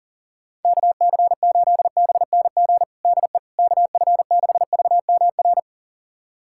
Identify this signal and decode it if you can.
Morse code, heard as a keyed tone, about 37 wpm, 690 Hz. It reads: KC8BNG DE KF6VMR
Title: USA Callsign DE Callsign - 6 Characters - 40wpm